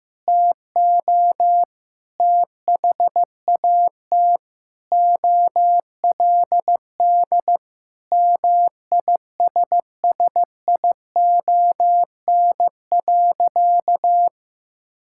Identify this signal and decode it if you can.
Morse code, heard as a keyed tone, about 15 wpm, 700 Hz. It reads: TO THAT OLD MISSION.
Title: Sentences from Top 1600 Words 15wpm